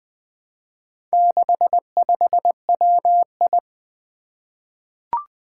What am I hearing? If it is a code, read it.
Morse code, heard as a keyed tone, about 20 wpm, 700 Hz. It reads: 65WI
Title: Four Challenging Letter-Number Combinations 20wpm